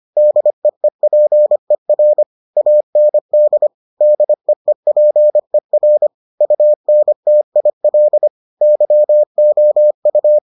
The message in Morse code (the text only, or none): DEEPER AND DEEPER UNTIL YOU